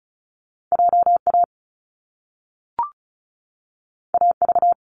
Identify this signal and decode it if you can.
Morse code, heard as a keyed tone, about 36 wpm, 700 Hz. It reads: JA AV
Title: Top Two Letter Combinations in Words 35wpm